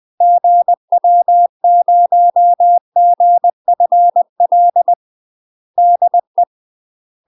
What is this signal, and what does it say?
Morse code, heard as a keyed tone, about 20 wpm, 700 Hz. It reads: GW0GFL DE
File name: UK MASTER QSOs TRAINING / 20WPM / UK MASTER QSOs 0073